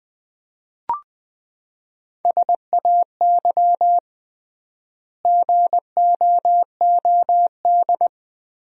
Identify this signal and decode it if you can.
Morse code, heard as a keyed tone, about 20 wpm, 700 Hz. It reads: SAY GOOD
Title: 3x Word-Spacing - Sentences from Top 300 Words - 20wpm